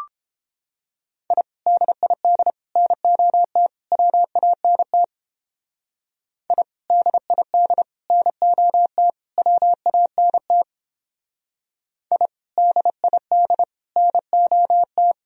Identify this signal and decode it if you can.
Morse code, heard as a keyed tone, about 29 wpm, 700 Hz. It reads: I DID NOT WANT I DID NOT WANT I DID NOT